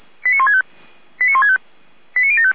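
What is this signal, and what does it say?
Five-tone selective call (ZVEI), heard as quick runs of five tones, tones of about 70 ms. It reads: 89165 89165 89075
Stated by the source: ZVEI five-tone selective call received on SDR.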